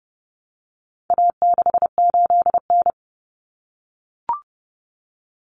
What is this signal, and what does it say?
Morse code, heard as a keyed tone, about 30 wpm, 700 Hz. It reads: A68N
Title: Common Cut Numbers - 4-Digit 30wpm